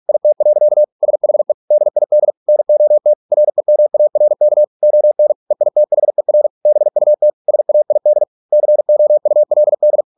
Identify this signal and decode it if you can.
Morse code, heard as a keyed tone, about 46 wpm, 600 Hz. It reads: IT. SHE DID NOT REMARK ON EITHER BUT SAID COULD